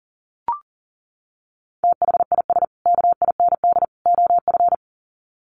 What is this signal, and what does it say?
Morse code, heard as a keyed tone, about 40 wpm, 700 Hz. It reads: THIS KIND OF
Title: Sets of 3 Words 40wpm